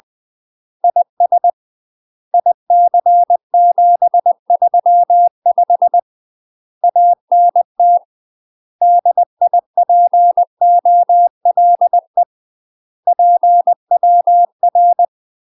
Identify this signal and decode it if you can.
Morse code, heard as a keyed tone, about 20 wpm, 700 Hz. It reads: IS IC735 ANT DIPOLE PWR
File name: UK MASTER QSOs TRAINING / 20WPM / UK MASTER QSOs 0003